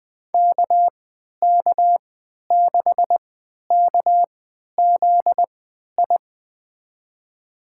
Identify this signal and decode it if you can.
Morse code, heard as a keyed tone, about 21 wpm, spaced as for 12 wpm, 700 Hz. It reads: KK6KZI
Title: USA Call Signs - 6 Characters - 3x Character Spacing - 20wpm